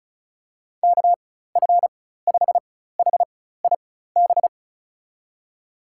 Morse code, heard as a keyed tone, about 36 wpm, spaced as for 16 wpm, 700 Hz. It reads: KF5HIB
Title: USA Call Signs - 6 Characters - 4x Character Spacing - 35wpm